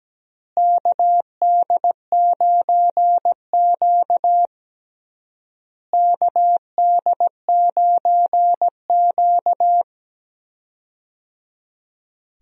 Morse code, heard as a keyed tone, about 17 wpm, 700 Hz. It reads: KD9Q KD9Q
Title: USA Call Signs - 4 Characters - Repeated 3x - 17wpm